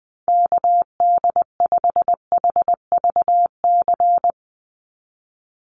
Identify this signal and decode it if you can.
Morse code, heard as a keyed tone, about 20 wpm, 700 Hz. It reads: KD5HVC
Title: USA Call Signs 20wpm